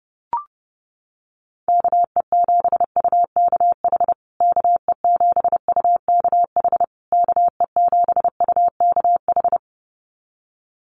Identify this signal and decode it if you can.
Morse code, heard as a keyed tone, about 30 wpm, 700 Hz. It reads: KE7UKH KE7UKH KE7UKH
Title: USA Call Signs Repeated 3x 30wpm